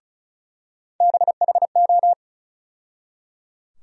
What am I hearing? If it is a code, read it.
Morse code, heard as a keyed tone, about 36 wpm, 700 Hz. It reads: BHO
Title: USA Call Sign Postfixes - 3 Letters 35wpm